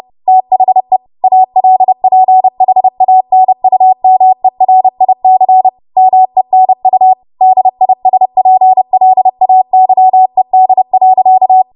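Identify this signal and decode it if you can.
Morse code, heard as a keyed tone, about 30 wpm, 750 Hz. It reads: THE ALPHANUMERIC MENU DISPLAYED.
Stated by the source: Morse practice transmission at 30 WPM.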